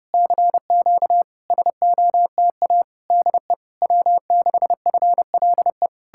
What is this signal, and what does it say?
Morse code, heard as a keyed tone, about 33 wpm, 700 Hz. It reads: CQ SOTA DE W6FLE